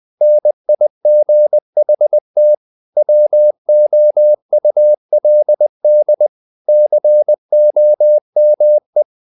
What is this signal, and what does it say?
Morse code, heard as a keyed tone, about 20 wpm, 600 Hz. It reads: NIGHT WOULD COME
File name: the old man and the sea / the old man and the sea part 1 20wpm